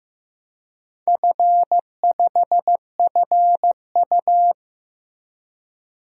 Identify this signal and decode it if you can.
Morse code, heard as a keyed tone, about 16 wpm, 700 Hz. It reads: F5FU
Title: DXCC-100 Call Signs - 4 Characters - Repeated 2x - 15wpm